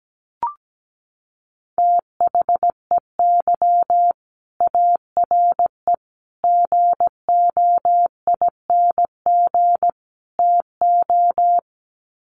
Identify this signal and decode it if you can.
Morse code, heard as a keyed tone, about 17 wpm, 700 Hz. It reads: THEY ARE GOING TO
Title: Sets of 4 Words 17wpm